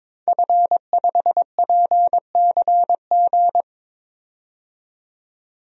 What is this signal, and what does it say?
Morse code, heard as a keyed tone, about 22 wpm, 700 Hz. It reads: F5PCG
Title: DXCC-100 Call Signs - 5 Characters - 22wpm